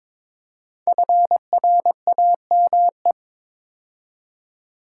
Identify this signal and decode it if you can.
Morse code, heard as a keyed tone, about 22 wpm, 700 Hz. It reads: FRAME
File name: 5 Letter Words - First 3 Letters Given - 22wpm